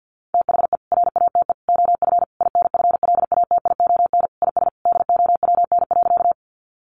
Morse code, heard as a keyed tone, about 50 wpm, 700 Hz. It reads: THE RATE OF INFLATION IS DOWN.
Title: Sentences from Top 1100 Words 50wpm